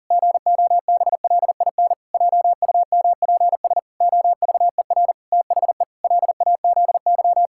Morse code, heard as a keyed tone, about 40 wpm, 700 Hz. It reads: GOBLIN JUMPS OVER THE LAZY
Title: Pangram Challenge v2 - 40wpm